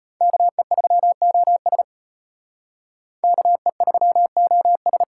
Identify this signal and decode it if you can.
Morse code, heard as a keyed tone, about 32 wpm, 700 Hz. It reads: KE3OS KE3OS